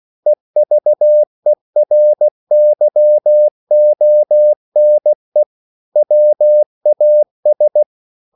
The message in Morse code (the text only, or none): EVERYONE WAS